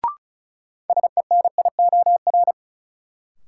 Morse code, heard as a keyed tone, about 36 wpm, 700 Hz. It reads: SENIOR
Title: Sets of 5 Words Encore - New Word Drill - 35wpm